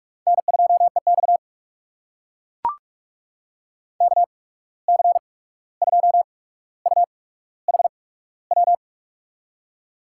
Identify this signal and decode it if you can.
Morse code, heard as a keyed tone, about 44 wpm, 690 Hz. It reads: N 2 E X K C 2 U H W
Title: USA Call Signs - 8x Character Spacing - 45wpm